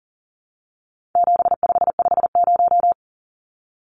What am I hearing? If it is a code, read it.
Morse code, heard as a keyed tone, about 40 wpm, 700 Hz. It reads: 7550
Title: Numbers - 4-Digit 40wpm